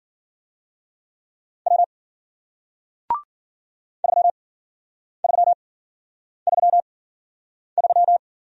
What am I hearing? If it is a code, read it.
Morse code, heard as a keyed tone, about 54 wpm, spaced as for 9 wpm, 700 Hz. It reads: W 3333